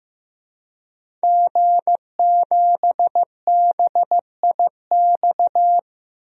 Morse code, heard as a keyed tone, about 15 wpm, 700 Hz. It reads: G7BIX
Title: DXCC-100 Call Signs - 5 Characters - 15wpm